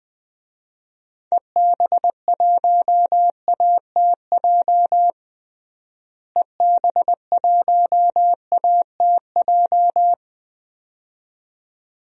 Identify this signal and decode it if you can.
Morse code, heard as a keyed tone, about 20 wpm, 700 Hz. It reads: EB1ATJ EB1ATJ
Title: DXCC-100 Call Signs - 6 Characters - Repeated 2x - 20wpm